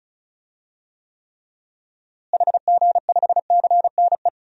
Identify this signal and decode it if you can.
Morse code, heard as a keyed tone, about 36 wpm, 700 Hz. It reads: HG5CNE
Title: DXCC-100 Call Signs - 6 Characters - Repeated 2x - 35wpm